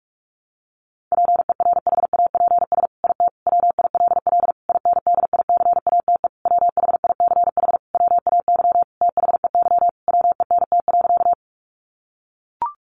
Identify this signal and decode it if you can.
Morse code, heard as a keyed tone, about 44 wpm, 700 Hz. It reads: PERHAPS IT WILL INDICATE WHICH WAY THEY WENT.
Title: Sentences from Top 1300 Words 45wpm